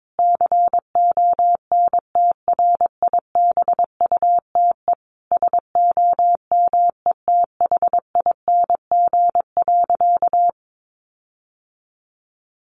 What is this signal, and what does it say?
Morse code, heard as a keyed tone, about 22 wpm, 700 Hz. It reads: CONTRIBUTE SOMETHING.
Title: Sentences from Top 1400 Words 22wpm